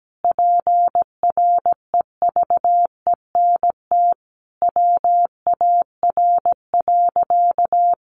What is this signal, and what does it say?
Morse code, heard as a keyed tone, about 17 wpm, 700 Hz. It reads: PREVENT WAR.
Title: Sentences from Top 1100 Words 17wpm